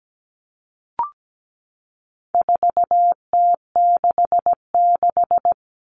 Morse code, heard as a keyed tone, about 17 wpm, 700 Hz. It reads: 4T66
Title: All Cut Numbers - 4-Digit 17wpm